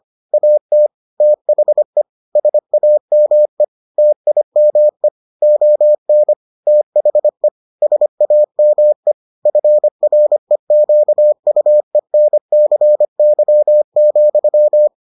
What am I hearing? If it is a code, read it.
Morse code, heard as a keyed tone, about 25 wpm, 600 Hz. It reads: AT THE SAME TIME ON THE SAME FREQUENCY,